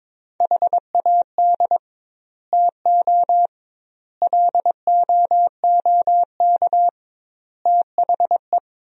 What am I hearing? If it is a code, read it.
Morse code, heard as a keyed tone, about 22 wpm, 700 Hz. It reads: HAD TO LOOK THE
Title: 2x Word-Spacing - Sentences from Top 100 Words - 22wpm